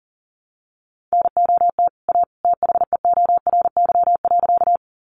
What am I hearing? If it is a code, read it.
Morse code, heard as a keyed tone, about 40 wpm, 700 Hz. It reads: NOT A THEORY.
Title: Sentences from Top 600 Words 40wpm